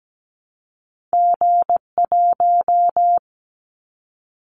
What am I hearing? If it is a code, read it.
Morse code, heard as a keyed tone, about 17 wpm, 700 Hz. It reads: G1
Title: Callsign Prefixes for Top 25 DXCC - Repeated 3x - 17wpm